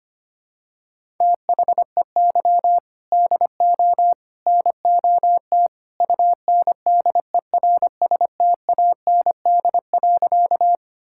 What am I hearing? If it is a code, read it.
Morse code, heard as a keyed tone, about 25 wpm, 700 Hz. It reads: THEY DO NOT UNDERSTAND.
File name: practice / 25wpm / Sentences with 3-Word Phrases - 25wpm